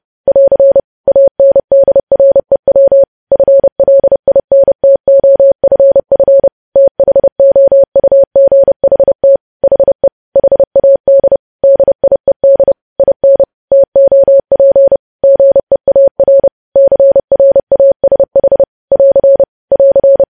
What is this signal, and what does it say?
Morse code, heard as a keyed tone, about 30 wpm, 550 Hz. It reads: <AR> ANDREW FLINTOFF THOUGHT HE HAD DIED IN TOP GEAR CRASH <AR> <AR>
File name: podcast / News headlines-1x-30x30